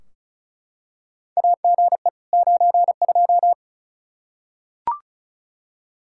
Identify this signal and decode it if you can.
Morse code, heard as a keyed tone, about 36 wpm, 700 Hz. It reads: AGE 92
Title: Practice QSO - Exchange 3 - More Info - 35wpm